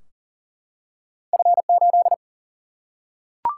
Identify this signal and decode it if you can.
Morse code, heard as a keyed tone, about 39 wpm, 700 Hz. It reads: F8
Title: Single-Letter Single-Number 40wpm